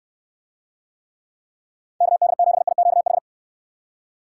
Morse code, heard as a keyed tone, about 67 wpm, 700 Hz. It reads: KD7IQH